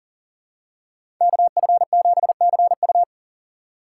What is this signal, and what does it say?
Morse code, heard as a keyed tone, about 40 wpm, 700 Hz. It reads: KF7CU
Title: USA Call Signs - 5 Characters - 3x Character Spacing - 40wpm